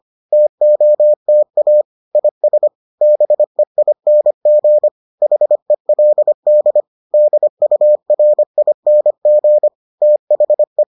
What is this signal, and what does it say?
Morse code, heard as a keyed tone, about 25 wpm, 600 Hz. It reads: TOTA IS BEING HELD DURING THE